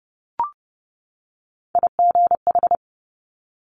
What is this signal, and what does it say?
Morse code, heard as a keyed tone, about 30 wpm, 700 Hz. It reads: IGH
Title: Top Three Letter Combinations in Words 30wpm